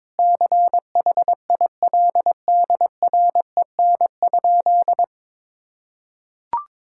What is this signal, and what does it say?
Morse code, heard as a keyed tone, about 21 wpm, 700 Hz. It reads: CHILDREN?
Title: Sentences from Top 1000 Words - Review Part 6 - 22wpm